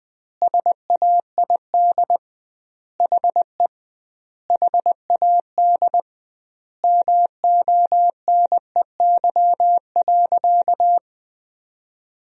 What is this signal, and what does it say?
Morse code, heard as a keyed tone, about 20 wpm, 700 Hz. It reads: SAID HE HAD MONEY.
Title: 2x Word-Spacing - Sentences from Top 500 Words - 20wpm